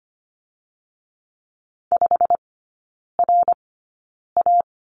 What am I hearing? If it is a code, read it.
Morse code, heard as a keyed tone, about 25 wpm, spaced as for 9 wpm, 690 Hz. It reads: 5RA
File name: Lesson 15e - Number 5 - Three Characters 25wpm